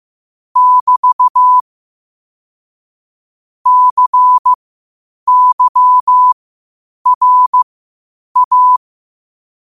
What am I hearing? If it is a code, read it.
Morse code, heard as a keyed tone, about 15 wpm, spaced as for 8 wpm, 1000 Hz. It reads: <BT> CYRA